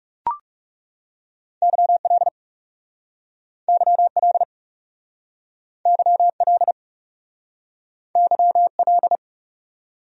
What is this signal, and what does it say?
Morse code, heard as a keyed tone, about 37 wpm, 700 Hz. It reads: YL YL YL YL
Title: Speed-Racing Abbreviations - 30wpm